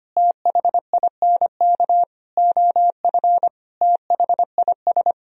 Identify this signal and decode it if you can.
Morse code, heard as a keyed tone, about 25 wpm, 700 Hz. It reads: THINK OF THIS
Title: Sentences from Top 1000 Words - Review Part 3 - 25wpm